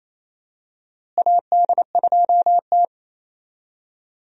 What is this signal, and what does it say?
Morse code, heard as a keyed tone, about 28 wpm, 700 Hz. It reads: AD2T
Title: USA Call Signs - 4 Characters - Repeated 3x - 28wpm